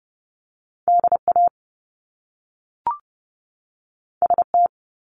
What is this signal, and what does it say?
Morse code, heard as a keyed tone, about 30 wpm, 700 Hz. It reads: DA ST
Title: Top Two Letter Combinations in Words 30wpm